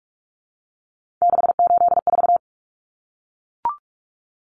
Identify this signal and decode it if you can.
Morse code, heard as a keyed tone, about 44 wpm, 700 Hz. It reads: 684
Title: Numbers - 3-Digit 45wpm